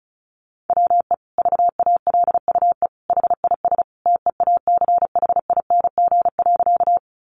Morse code, heard as a keyed tone, about 34 wpm, 700 Hz. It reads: WE VALUE HIS TEACHING.
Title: Sentences from Top 1100 Words 35wpm